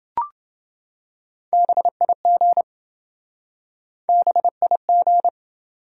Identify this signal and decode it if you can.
Morse code, heard as a keyed tone, about 27 wpm, 700 Hz. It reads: BIG BIG